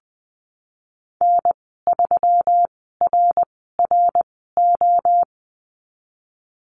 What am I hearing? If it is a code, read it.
Morse code, heard as a keyed tone, about 20 wpm, spaced as for 15 wpm, 700 Hz. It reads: N3RRO
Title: 161 Lesson 26g - Letter g - Call Signs 20wpm